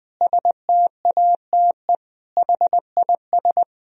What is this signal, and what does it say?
Morse code, heard as a keyed tone, about 20 wpm, 700 Hz. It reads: STATE HIS
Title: Sentences from Top 1000 Words - Review Part 2 - 20wpm